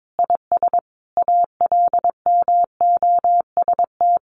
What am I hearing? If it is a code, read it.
Morse code, heard as a keyed tone, about 22 wpm, 700 Hz. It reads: IS ALMOST
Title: Sentences from Top 1300 Words 22wpm